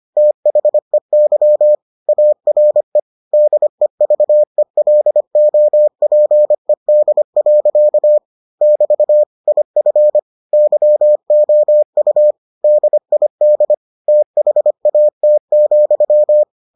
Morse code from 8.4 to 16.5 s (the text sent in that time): <BT> IF YOU DID THAT,